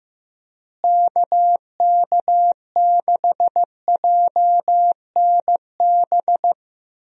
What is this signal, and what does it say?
Morse code, heard as a keyed tone, about 15 wpm, 700 Hz. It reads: KK6JNB
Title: USA Call Signs - 6 Characters - Repeated 3x - 15wpm